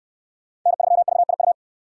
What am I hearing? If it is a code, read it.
Morse code, heard as a keyed tone, about 62 wpm, 700 Hz. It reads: N3VIU